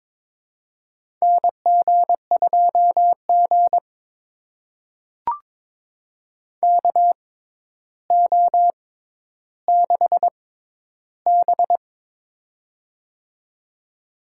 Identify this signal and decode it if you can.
Morse code, heard as a keyed tone, about 22 wpm, spaced as for 11 wpm, 700 Hz. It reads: NG2G KO6B
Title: USA Call Signs - 4 Characters - 6x Character Spacing - 22wpm